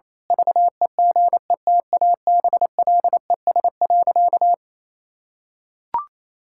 Morse code, heard as a keyed tone, about 27 wpm, 700 Hz. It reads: VEGETABLES.
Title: Pangram Challenge v2 - 28wpm